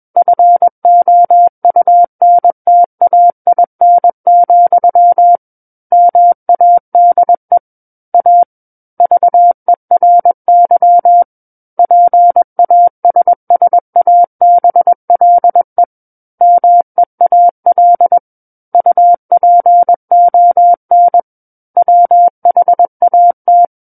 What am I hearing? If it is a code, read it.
Morse code, heard as a keyed tone, about 21 wpm, 700 Hz. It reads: FOUNTAIN, MADE A VERY PASSABLE MEAL UPON WHAT
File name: Book One / 12-Chapter 12 - 21 WPM